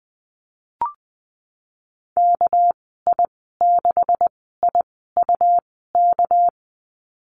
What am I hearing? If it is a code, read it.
Morse code, heard as a keyed tone, about 20 wpm, 700 Hz. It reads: K I 6 I U K
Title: 200 Lesson 32g - Letter k - Call Signs 20wpm